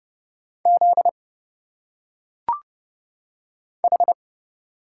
Morse code, heard as a keyed tone, about 29 wpm, 700 Hz. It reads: Z H
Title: Single Letters - Phonetic Pronunciation 30wpm